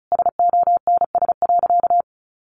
Morse code, heard as a keyed tone, about 34 wpm, 700 Hz. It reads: SONS.
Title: A Sound Of Thunder 35wpm